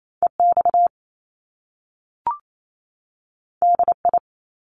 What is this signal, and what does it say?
Morse code, heard as a keyed tone, about 28 wpm, 700 Hz. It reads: EX DI
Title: Top Two Letter Combinations in Words 28wpm